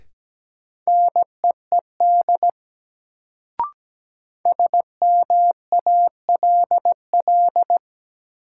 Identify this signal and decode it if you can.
Morse code, heard as a keyed tone, about 17 wpm, 700 Hz. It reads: NEED SMALL
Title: Sets of 2 Words - New Word Drill - 17wpm